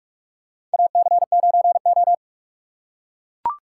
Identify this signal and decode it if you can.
Morse code, heard as a keyed tone, about 44 wpm, 700 Hz. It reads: AC9O
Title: USA Call Signs - 4 Characters - 3x Character Spacing - 45wpm